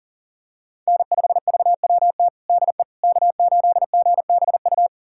Morse code, heard as a keyed tone, about 37 wpm, 700 Hz. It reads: N5VWT DE K8GBU
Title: USA Callsign DE Callsign - 5 Characters - 40wpm